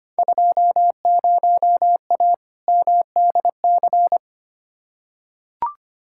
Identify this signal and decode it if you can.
Morse code, heard as a keyed tone, about 25 wpm, 700 Hz. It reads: 20A MDC
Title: Field Day Exchanges Encore - 25wpm